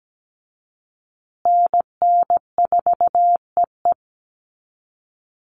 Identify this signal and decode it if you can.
Morse code, heard as a keyed tone, about 17 wpm, 700 Hz. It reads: NN4EE
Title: USA Call Signs Repeated 2x 17wpm